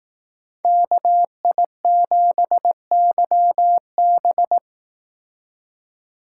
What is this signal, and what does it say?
Morse code, heard as a keyed tone, about 18 wpm, 700 Hz. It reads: KI7YB
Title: Speed Racing - USA Call Signs 15wpm